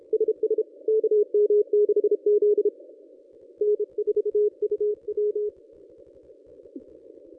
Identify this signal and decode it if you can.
Morse code, heard as a keyed tone, about 33 wpm, 420 Hz. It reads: SS KM6Z N4UW
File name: contests / ss-cw-k5tr-op-n5rz -1100